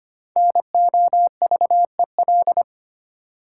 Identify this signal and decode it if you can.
Morse code, heard as a keyed tone, about 25 wpm, 700 Hz. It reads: NOVEL
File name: practice / 25wpm / Difficult Words with Hints - 25wpm